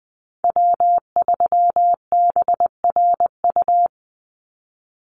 Morse code, heard as a keyed tone, about 20 wpm, 700 Hz. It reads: W3BRU
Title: USA Call Signs 20wpm